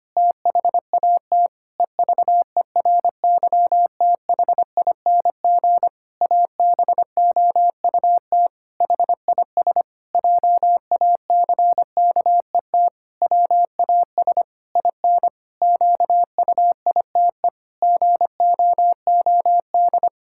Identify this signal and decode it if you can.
Morse code, heard as a keyed tone, about 25 wpm, 700 Hz. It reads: THAT EVERYTHING ABOUT HIS JACKET WAS IN QUITE GOOD
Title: Pangram Challenge v2 - 25wpm